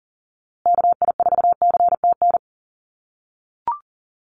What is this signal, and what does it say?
Morse code, heard as a keyed tone, about 40 wpm, 700 Hz. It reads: KI4CTN
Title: USA Call Signs Repeated 2x 40wpm